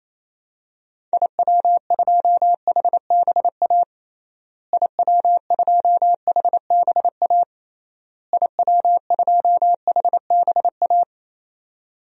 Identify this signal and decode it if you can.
Morse code, heard as a keyed tone, about 28 wpm, 700 Hz. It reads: IW2HBA IW2HBA IW2HBA
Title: DXCC-100 Call Signs - 6 Characters - Repeated 3x - 28wpm